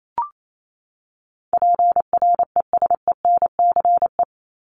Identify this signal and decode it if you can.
Morse code, heard as a keyed tone, about 28 wpm, 700 Hz. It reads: PRESENCE
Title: Top 1751-2000 Words 28wpm